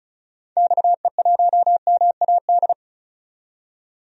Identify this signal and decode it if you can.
Morse code, heard as a keyed tone, about 34 wpm, 700 Hz. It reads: XE1MAD
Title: DXCC-100 Call Signs - 6 Characters - 35wpm